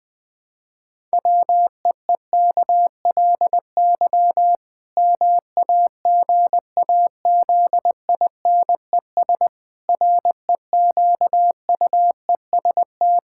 Morse code, heard as a keyed tone, about 20 wpm, 700 Hz. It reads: WEEKLY MAGAZINES REQUEST
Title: Pangram Challenge v2 - 20wpm